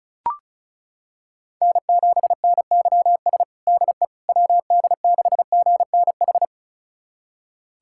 Morse code, heard as a keyed tone, about 36 wpm, 700 Hz. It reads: N7NYS DE WD6GNH
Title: USA Callsign DE Callsign 35wpm